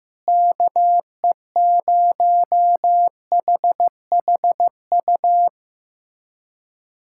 Morse code, heard as a keyed tone, about 16 wpm, 700 Hz. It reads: KE0HHU
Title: USA Callsign DE Callsign 15wpm